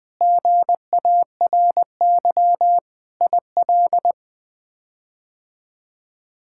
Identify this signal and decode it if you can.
Morse code, heard as a keyed tone, about 20 wpm, 700 Hz. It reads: GARY IL
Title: K1USN SST QSO Practice - 20wpm